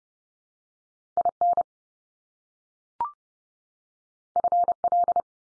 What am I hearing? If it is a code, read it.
Morse code, heard as a keyed tone, about 30 wpm, 700 Hz. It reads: IN FL
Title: US States 30WPM